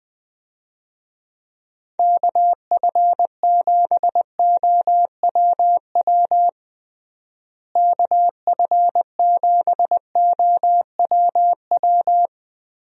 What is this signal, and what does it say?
Morse code, heard as a keyed tone, about 20 wpm, 700 Hz. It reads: KF7OWW KF7OWW
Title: USA Call Signs - 6 Characters - Repeated 3x - 20wpm